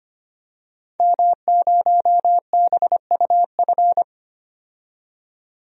Morse code, heard as a keyed tone, about 25 wpm, 700 Hz. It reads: M0BUF
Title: DXCC-100 Call Signs - 5 Characters - 25wpm